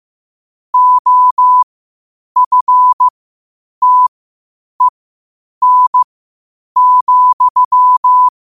Morse code, heard as a keyed tone, about 15 wpm, spaced as for 8 wpm, 1000 Hz. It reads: OFTEN,
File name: qotdfiles / 8wpm